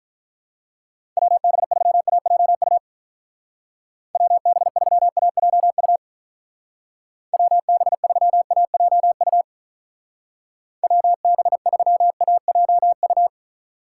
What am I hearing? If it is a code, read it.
Morse code, heard as a keyed tone, about 45 wpm, 700 Hz. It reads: WB3AJU WB3AJU WB3AJU WB3AJU